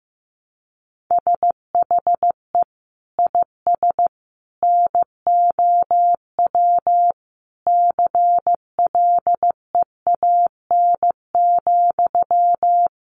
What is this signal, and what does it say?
Morse code, heard as a keyed tone, about 15 wpm, 700 Hz. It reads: SHE IS NOW CLEAN,
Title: Sentences from Top 1100 Words 15wpm